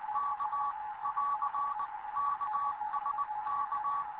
Morse code, heard as a keyed tone, about 20 wpm, 1140 Hz, with radio noise. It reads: K <AR> KSK
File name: Morze-HA2MN / hangp / 2 11 GYAKORLAT